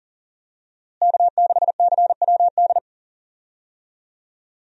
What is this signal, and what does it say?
Morse code, heard as a keyed tone, about 40 wpm, 700 Hz. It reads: K6CWD
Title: USA Call Signs - 5 Characters - Repeated 2x - 40wpm